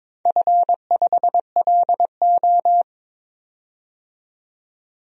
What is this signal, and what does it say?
Morse code, heard as a keyed tone, about 21 wpm, 700 Hz. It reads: F5LO
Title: DXCC-100 Call Signs - 4 Characters - 22wpm